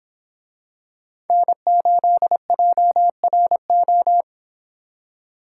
Morse code, heard as a keyed tone, about 26 wpm, 700 Hz. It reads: N8JRO